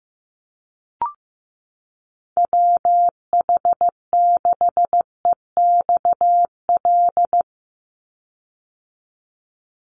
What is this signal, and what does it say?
Morse code, heard as a keyed tone, about 15 wpm, 700 Hz. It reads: WH6EXL
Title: USA Call Signs 15wpm